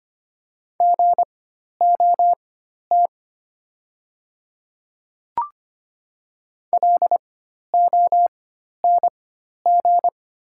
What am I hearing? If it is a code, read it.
Morse code, heard as a keyed tone, about 25 wpm, spaced as for 12 wpm, 690 Hz. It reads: GOT LONG
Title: 4x Character Spacing - Top 300 Words - 25wpm